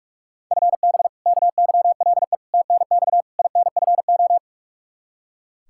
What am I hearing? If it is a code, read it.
Morse code, heard as a keyed tone, about 44 wpm, 700 Hz. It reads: FB KYLE TNX INFO
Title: Practice QSO - Exchange 3 - More Info - 45wpm